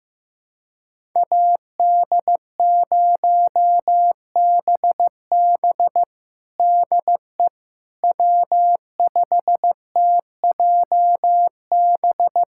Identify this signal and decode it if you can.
Morse code, heard as a keyed tone, about 15 wpm, 700 Hz. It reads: AD0BB DE W5TJB
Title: USA Callsign DE Callsign - 5 Characters - 15wpm